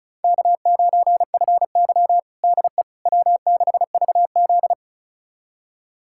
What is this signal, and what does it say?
Morse code, heard as a keyed tone, about 36 wpm, 700 Hz. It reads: K9FY DE W6VZ
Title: USA Callsign DE Callsign - 4 Characters - 35wpm